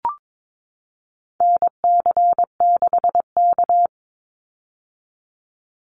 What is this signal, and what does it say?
Morse code, heard as a keyed tone, about 22 wpm, 700 Hz. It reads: NC6K
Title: Call Signs - 22wpm